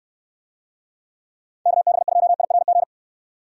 Morse code, heard as a keyed tone, about 68 wpm, 700 Hz. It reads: KB2IRG